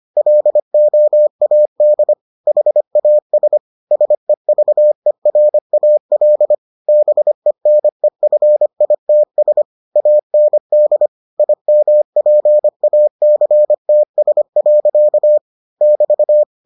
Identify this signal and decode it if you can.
Morse code, heard as a keyed tone, about 25 wpm, 600 Hz. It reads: LOAD HAS SEVERAL BENEFITS AND IMPACTS. <BT>